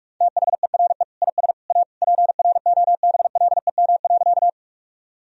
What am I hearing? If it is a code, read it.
Morse code, heard as a keyed tone, about 46 wpm, 700 Hz. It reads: THERE IS A PROBLEM.
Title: Sentences from Top 1000 Words - Review Part 2 - 45wpm